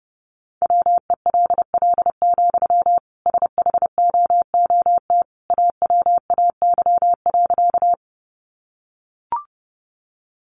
Morse code, heard as a keyed tone, about 29 wpm, 700 Hz. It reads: WELL, SHOOT AWAY.
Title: Sentences from Top 1200 Words 30wpm